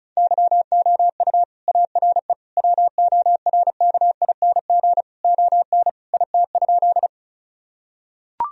Morse code, heard as a keyed tone, about 34 wpm, 700 Hz. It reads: YOU ARE WORKING ON IT?
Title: Sentences from Top 1000 Words - Review Part 4 - 35wpm